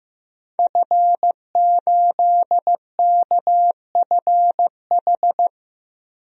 Morse code, heard as a keyed tone, about 15 wpm, 700 Hz. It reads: F8KFH
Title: DXCC-100 Call Signs - 5 Characters - Repeated 2x - 15wpm